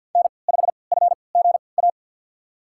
Morse code, heard as a keyed tone, about 52 wpm, spaced as for 29 wpm, 700 Hz. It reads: N5FGA
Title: USA Call Signs - 5 Characters - 3x Character Spacing - 50wpm